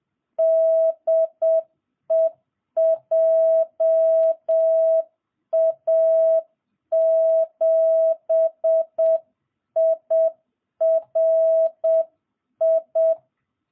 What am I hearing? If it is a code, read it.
Morse code, heard as a keyed tone, about 7 wpm, 650 Hz. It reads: DE JA7IRI